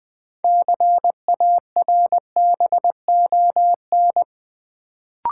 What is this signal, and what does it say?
Morse code, heard as a keyed tone, about 20 wpm, 700 Hz. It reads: CARBON
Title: Build A Word - 2 Letters At A Time - 6-Letter Words - 20wpm